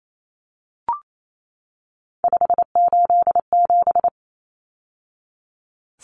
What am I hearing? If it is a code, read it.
Morse code, heard as a keyed tone, about 28 wpm, 700 Hz. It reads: 587
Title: Numbers - 3-Digit 28wpm